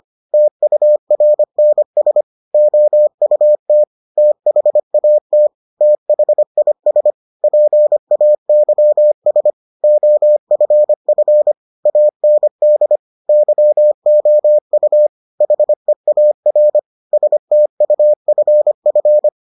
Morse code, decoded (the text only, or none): TURNS OUT THAT THIS PAYS OFF AND YOU HEAR STUFF